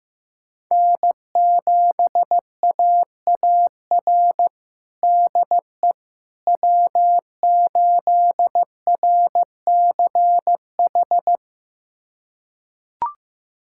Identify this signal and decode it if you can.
Morse code, heard as a keyed tone, about 16 wpm, 700 Hz. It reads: N7AAR DE W8RCH
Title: USA Callsign DE Callsign - 5 Characters - 15wpm